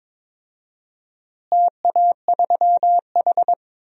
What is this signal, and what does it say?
Morse code, heard as a keyed tone, about 22 wpm, 700 Hz. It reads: TA3H
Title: DXCC-100 Call Signs - 4 Characters - Repeated 3x - 22wpm